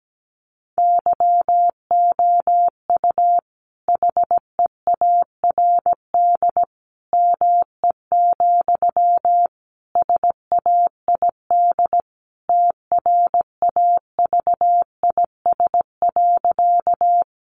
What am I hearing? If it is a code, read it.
Morse code, heard as a keyed tone, about 17 wpm, 700 Hz. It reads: YOU HEARD ME, SAID TRAVIS.
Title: A Sound Of Thunder 17wpm